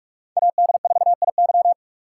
Morse code, heard as a keyed tone, about 51 wpm, 700 Hz. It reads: AD4IY
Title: Speed Racing - USA 2x2 Call Signs 45wpm